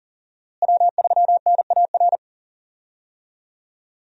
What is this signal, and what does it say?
Morse code, heard as a keyed tone, about 40 wpm, 700 Hz. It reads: W3NAR
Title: USA Call Signs - 5 Characters - Repeated 3x - 40wpm